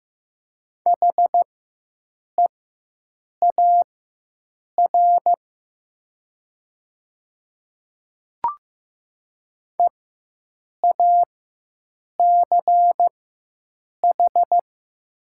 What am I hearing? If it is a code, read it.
Morse code, heard as a keyed tone, about 15 wpm, spaced as for 7 wpm, 700 Hz. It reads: HEAR EACH
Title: 4x Character Spacing - Top 500 Words - 15wpm